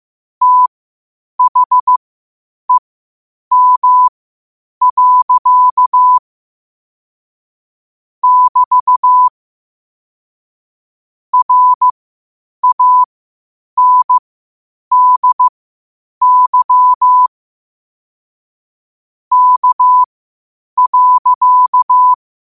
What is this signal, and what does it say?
Morse code, heard as a keyed tone, about 15 wpm, spaced as for 8 wpm, 1000 Hz. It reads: THEM. <BT> RANDY K.
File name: qotdfiles / 8wpm